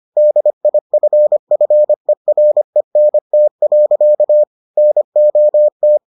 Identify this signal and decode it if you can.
Morse code, heard as a keyed tone, about 25 wpm, 600 Hz. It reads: DIFFERENT. NOT